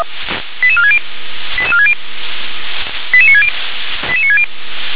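Five-tone selective call (ZVEI), heard as quick runs of five tones, tones of about 80 ms. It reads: 88460 88460 88960 88960
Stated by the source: ZVEI five-tone selective call 88966, received by SDR.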